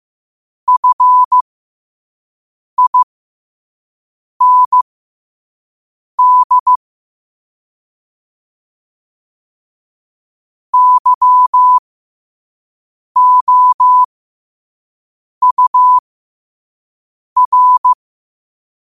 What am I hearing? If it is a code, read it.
Morse code, heard as a keyed tone, about 15 wpm, spaced as for 5 wpm, 1000 Hz. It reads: FIND YOUR
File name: qotdfiles / 5wpm